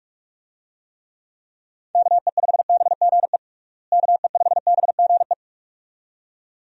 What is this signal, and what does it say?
Morse code, heard as a keyed tone, about 46 wpm, 700 Hz. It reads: KE5BGE KE5BGE
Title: USA Call Signs - 6 Characters - Repeated 2x - 45wpm